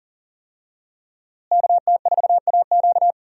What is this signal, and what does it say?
Morse code, heard as a keyed tone, about 40 wpm, 700 Hz. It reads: KT4AQ
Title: USA Call Signs - 5 Characters - Repeated 3x - 40wpm